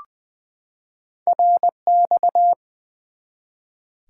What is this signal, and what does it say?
Morse code, heard as a keyed tone, about 20 wpm, 700 Hz. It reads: RX
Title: Top 90 QSO Elements 20wpm